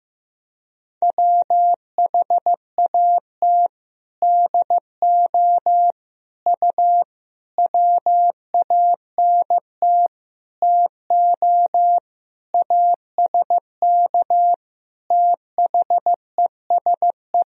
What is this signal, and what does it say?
Morse code, heard as a keyed tone, about 15 wpm, 700 Hz. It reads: WHAT DO U WANT TO ASK THESE
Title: Sentences from Top 200 Words and Abbreviations 15wpm